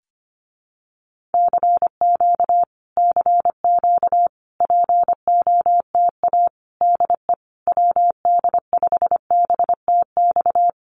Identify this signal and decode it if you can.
Morse code, heard as a keyed tone, about 25 wpm, 700 Hz. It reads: CQ CQ POTA DE WD5BTX
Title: Common QSO Phrases Encore 25wpm